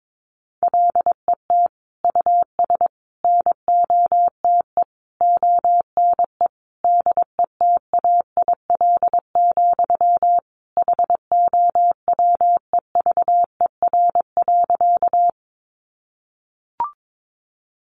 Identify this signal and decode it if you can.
Morse code, heard as a keyed tone, about 22 wpm, 700 Hz. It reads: LET US NOTE ONE DETAIL, HOWEVER.
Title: Sentences from Top 1500 Words 22wpm